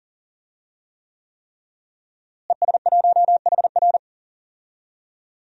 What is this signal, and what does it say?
Morse code, heard as a keyed tone, about 38 wpm, 700 Hz. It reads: ES1HR
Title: DXCC-100 Call Signs - 5 Characters - Repeated 3x - 40wpm